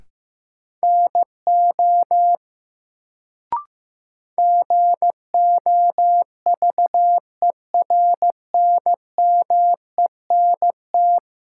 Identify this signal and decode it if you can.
Morse code, heard as a keyed tone, about 15 wpm, 700 Hz. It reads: NO GOVERNMENT
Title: Sets of 2 Words - New Word Drill - 15wpm